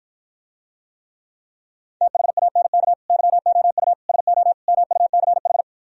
Morse code, heard as a keyed tone, about 52 wpm, 700 Hz. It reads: THANK YOU SO MUCH